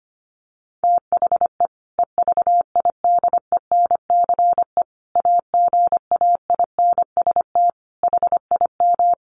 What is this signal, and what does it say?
Morse code, heard as a keyed tone, about 25 wpm, 700 Hz. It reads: THE EVIDENCE AGAINST HIM
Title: Sentences from Top 1300 Words 25wpm